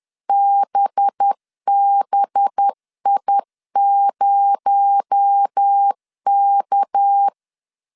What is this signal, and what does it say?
Morse code, heard as a keyed tone, about 10 wpm, 800 Hz. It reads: BBI0K
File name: CW / DET / 1 6 1500C S B 10wpm 4137